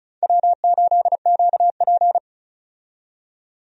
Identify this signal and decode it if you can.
Morse code, heard as a keyed tone, about 35 wpm, 700 Hz. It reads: W8QP
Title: USA Call Signs - 4 Characters - 35wpm